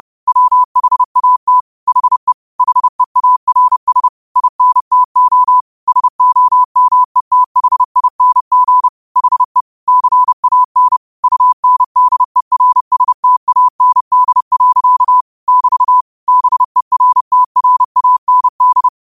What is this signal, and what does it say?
Morse code, heard as a keyed tone, about 30 wpm, 1000 Hz. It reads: WHAT HE HEARS INTO SOMETHING HE CAN UNDERSTAND. <BT> BERTRAND
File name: qotdfiles / 30wpm